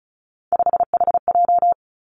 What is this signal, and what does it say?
Morse code, heard as a keyed tone, about 36 wpm, 700 Hz. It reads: 5HJ
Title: Three Challenging Letter-Number Combinations 35wpm